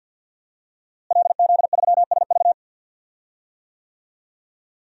Morse code, heard as a keyed tone, about 47 wpm, 700 Hz. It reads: LZ3SV
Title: DXCC-100 Call Signs - 5 Characters - Repeated 2x - 50wpm